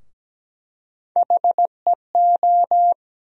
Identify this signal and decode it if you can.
Morse code, heard as a keyed tone, about 17 wpm, 700 Hz. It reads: HEO
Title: USA Call Sign Postfixes - 3 Letters 17wpm